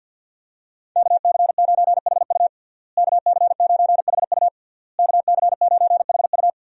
Morse code, heard as a keyed tone, about 52 wpm, 690 Hz. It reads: KC9HU KC9HU KC9HU
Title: USA Call Signs - 5 Characters - Repeated 3x - 50wpm